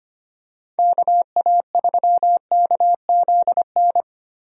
Morse code, heard as a keyed tone, about 25 wpm, 700 Hz. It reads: KA3KZN
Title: USA Call Signs - 6 Characters - Repeated 3x - 25wpm